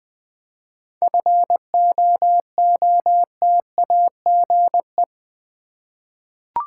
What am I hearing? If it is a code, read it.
Morse code, heard as a keyed tone, about 20 wpm, 690 Hz. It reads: FOOTAGE
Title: 7 Letter Words - First 3 Letters Given - 20wpm